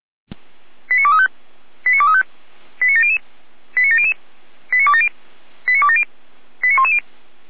Five-tone selective call (ZVEI), heard as quick runs of five tones, tones of about 80 ms. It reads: 89235 89235 89700 89700 89270 89270 89110